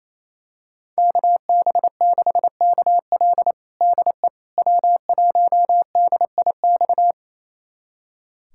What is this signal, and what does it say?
Morse code, heard as a keyed tone, about 28 wpm, 700 Hz. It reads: KB6KL DE W1DIX
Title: USA Callsign DE Callsign 28wpm